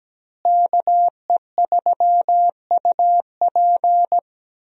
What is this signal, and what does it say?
Morse code, heard as a keyed tone, about 17 wpm, 700 Hz. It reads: KE3UP
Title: USA Call Signs - 5 Characters - Repeated 3x - 17wpm